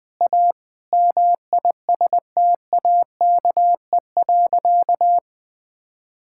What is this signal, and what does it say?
Morse code, heard as a keyed tone, about 20 wpm, 700 Hz. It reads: A MISTAKE.
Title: Sentences with 3-Word Phrases - 20wpm